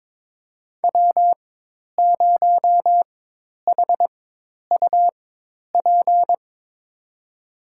Morse code, heard as a keyed tone, about 22 wpm, spaced as for 10 wpm, 700 Hz. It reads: W0HUP
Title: USA Call Signs - 4x Character Spacing - 22wpm